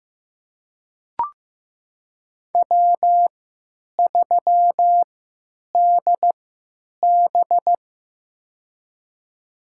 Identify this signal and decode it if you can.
Morse code, heard as a keyed tone, about 15 wpm, spaced as for 9 wpm, 700 Hz. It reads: W3DB
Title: USA Call Signs - 3x Character Spacing - 15wpm